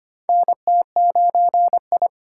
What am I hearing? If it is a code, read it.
Morse code, heard as a keyed tone, about 25 wpm, 700 Hz. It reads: NT9I
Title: USA Call Signs - 2x Character Spacing - 25wpm